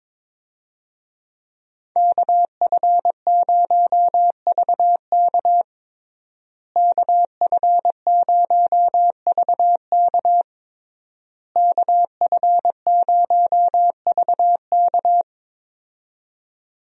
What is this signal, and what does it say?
Morse code, heard as a keyed tone, about 22 wpm, 700 Hz. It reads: KF0VK KF0VK KF0VK
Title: USA Call Signs - 5 Characters - Repeated 3x - 22wpm